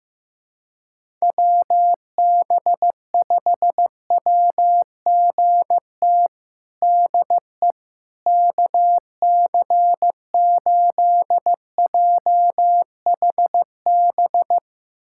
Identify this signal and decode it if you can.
Morse code, heard as a keyed tone, about 16 wpm, 700 Hz. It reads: WB5WGT DE KC8JHB
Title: USA Callsign DE Callsign 15wpm